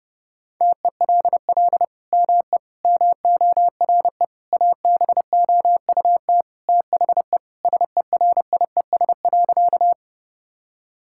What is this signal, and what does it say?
Morse code, heard as a keyed tone, about 32 wpm, 700 Hz. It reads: TELL ME MORE ABOUT THE SERIES.
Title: Sentences from Top 1000 Words - Review Part 5 - 30wpm